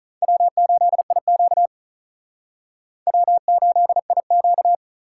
Morse code, heard as a keyed tone, about 39 wpm, 700 Hz. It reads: W8IQ W8IQ
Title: Speed Racing - USA 1x2 Call Signs 35wpm